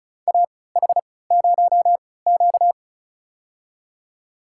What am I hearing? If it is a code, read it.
Morse code, heard as a keyed tone, about 36 wpm, spaced as for 20 wpm, 700 Hz. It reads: AH0Q
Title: USA Call Signs - 4 Characters - 3x Character Spacing - 35wpm